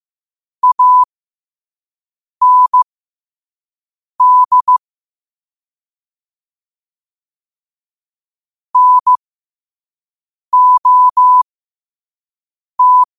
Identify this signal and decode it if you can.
Morse code, heard as a keyed tone, about 15 wpm, spaced as for 5 wpm, 1000 Hz. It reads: AND NOT
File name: qotdfiles / 5wpm